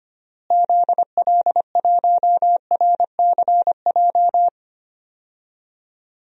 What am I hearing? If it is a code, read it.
Morse code, heard as a keyed tone, about 25 wpm, 700 Hz. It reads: ZL1RCJ
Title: DXCC-100 Call Signs - 6 Characters - Repeated 2x - 25wpm